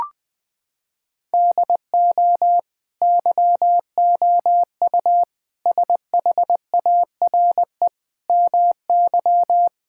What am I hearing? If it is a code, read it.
Morse code, heard as a keyed tone, about 20 wpm, 700 Hz. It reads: DO YOU SHARE MY
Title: Sentences from Top 1600 Words 20wpm